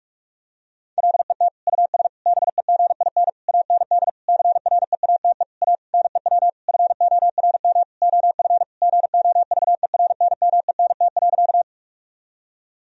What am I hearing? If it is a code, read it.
Morse code, heard as a keyed tone, about 42 wpm, 700 Hz. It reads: LET US BEGIN AND CREATE A NEW FORM OF GOVERNMENT.
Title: Sentences from Top 1000 Words - Review Part 3 - 45wpm